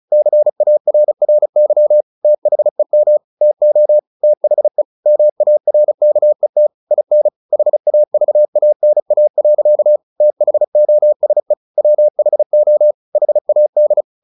Morse code, heard as a keyed tone, about 34 wpm, 600 Hz. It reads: CARRY THEM TO THE MARKET IN HAVANA. THOSE WHO HAD
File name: the old man and the sea / the old man and the sea 1 35wpm